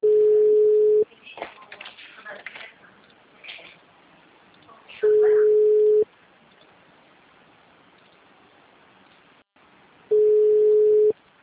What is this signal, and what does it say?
Ringback tone (Europe, 425 Hz), with speech.